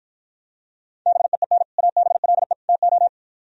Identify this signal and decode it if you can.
Morse code, heard as a keyed tone, about 52 wpm, 700 Hz. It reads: BEEN ABLE TO